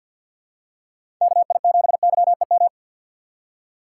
Morse code, heard as a keyed tone, about 47 wpm, 700 Hz. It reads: KI7YEM
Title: USA Call Signs - 6 Characters - 3x Character Spacing - 50wpm